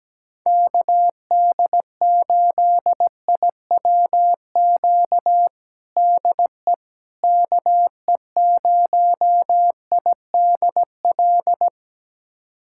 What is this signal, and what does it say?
Morse code, heard as a keyed tone, about 17 wpm, 700 Hz. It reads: KD8IWQ DE KE0IDL
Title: USA Callsign DE Callsign - 6 Characters - 17wpm